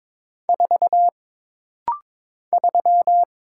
Morse code, heard as a keyed tone, about 22 wpm, 700 Hz. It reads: 4 3
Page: This rapid-fire timing is geared to push your ICR (Instant Character Recognition) skills.